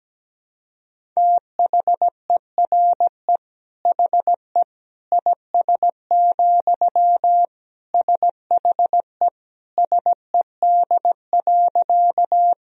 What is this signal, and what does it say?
Morse code, heard as a keyed tone, about 17 wpm, 700 Hz. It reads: THERE HE IS, SHE SED.
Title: Sentences from Top 100 Words and Abbreviations 17wpm